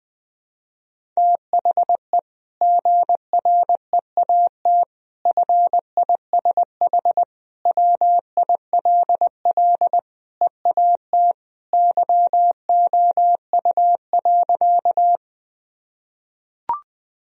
Morse code, heard as a keyed tone, about 21 wpm, 700 Hz. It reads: THE GREAT FISH WILL EAT YOU.
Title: Sentences from Top 1000 Words - Review Part 1 - 20wpm